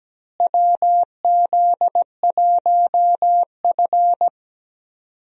Morse code, heard as a keyed tone, about 17 wpm, 700 Hz. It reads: WZ1F
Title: USA Call Signs - 4 Characters - Repeated 3x - 17wpm